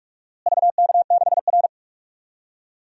Morse code, heard as a keyed tone, about 43 wpm, 700 Hz. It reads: VK6R
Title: DXCC-100 Call Signs - 4 Characters - Repeated 2x - 45wpm